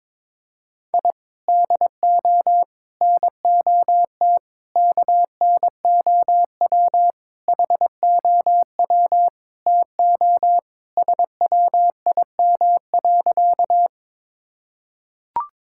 Morse code, heard as a keyed tone, about 22 wpm, 700 Hz. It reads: I DO NOT KNOW HOW TO SWIM.
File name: practice / 22wpm / Sentences with 4-Word Phrases - 22wpm